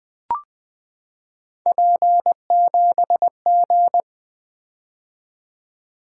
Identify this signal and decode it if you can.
Morse code, heard as a keyed tone, about 21 wpm, 700 Hz. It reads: P7G
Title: Three Letter-Number Combinations - 20wpm